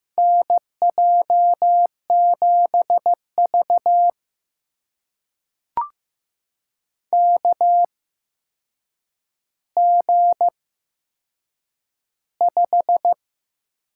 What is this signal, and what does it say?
Morse code, heard as a keyed tone, about 15 wpm, spaced as for 7 wpm, 700 Hz. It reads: NJ7V KG5
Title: USA Call Signs - 8x Character Spacing - 15wpm